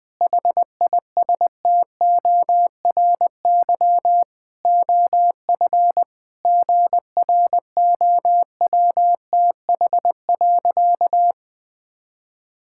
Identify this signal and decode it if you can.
Morse code, heard as a keyed tone, about 20 wpm, 700 Hz. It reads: HISTORY OF GROWTH.
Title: Sentences from Top 1000 Words - Review Part 5 - 20wpm